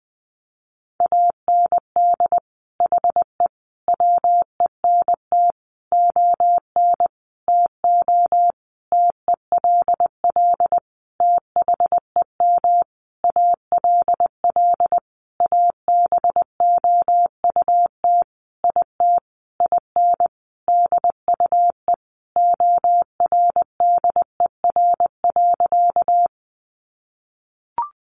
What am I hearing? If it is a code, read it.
Morse code, heard as a keyed tone, about 20 wpm, 700 Hz. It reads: AND HE WENT ON TO TELL THEM ALL ABOUT IT IN DUE ORDER.
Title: Sentences from Top 900 Words 20wpm